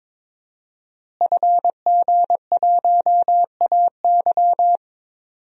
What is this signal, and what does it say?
Morse code, heard as a keyed tone, about 22 wpm, 700 Hz. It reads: FG1AY
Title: DXCC-100 Call Signs - 5 Characters - Repeated 3x - 22wpm